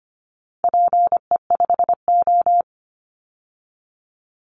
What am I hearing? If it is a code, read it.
Morse code, heard as a keyed tone, about 25 wpm, 700 Hz. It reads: PE5O
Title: Call Signs Repeated 3x - 25wpm